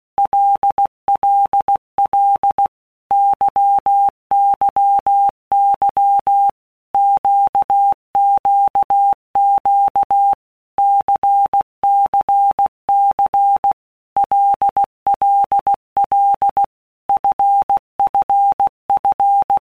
Morse code, heard as a keyed tone, about 16 wpm, 800 Hz. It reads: LLL YYY QQQ CCC LLL FFF
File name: Basic Letter Groups By Blocks / Groups@15 / LFCYQ@15